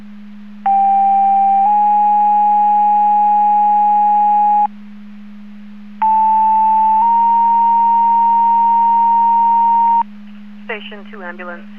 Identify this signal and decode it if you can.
Two-tone page, heard as a one-second tone, then a three-second tone, about 790 then 830 Hz, about 880 then 930 Hz, with speech. It is heard from 0.7 to 10.0 s.